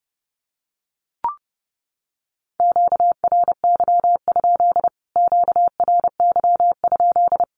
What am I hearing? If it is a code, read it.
Morse code, heard as a keyed tone, about 30 wpm, 700 Hz. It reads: QRY? QRY?
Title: Q-Codes Repeated 3x 30wpm